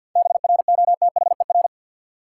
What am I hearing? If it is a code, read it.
Morse code, heard as a keyed tone, about 50 wpm, 700 Hz. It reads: BROTHER
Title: Seven Letter Words - 50wpm